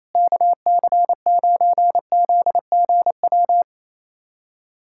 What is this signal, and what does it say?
Morse code, heard as a keyed tone, about 28 wpm, 700 Hz. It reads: KC9ZGW
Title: USA Callsign DE Callsign 28wpm